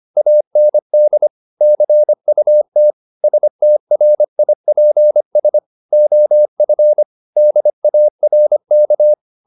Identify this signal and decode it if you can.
Morse code, heard as a keyed tone, about 25 wpm, 600 Hz. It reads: AND CUT STRIPS OF DARK
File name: the old man and the sea part 1 25wpm